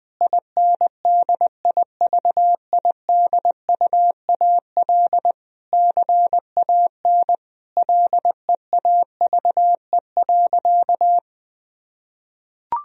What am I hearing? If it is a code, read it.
Morse code, heard as a keyed tone, about 21 wpm, 700 Hz. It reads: INDIVIDUAL CAN LEAVE.
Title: Sentences from Top 1000 Words - Review Part 4 - 20wpm